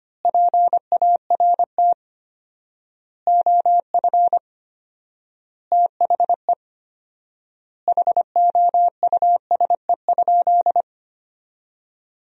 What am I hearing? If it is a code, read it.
Morse code, heard as a keyed tone, about 25 wpm, 700 Hz. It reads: PART OF THE HOUSE?
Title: 4x Word-Spacing - Sentences from Top 200 Words - 25wpm